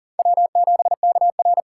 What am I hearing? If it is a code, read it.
Morse code, heard as a keyed tone, about 50 wpm, 700 Hz. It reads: W7KR
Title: Speed Racing - USA 1x2 Call Signs 40wpm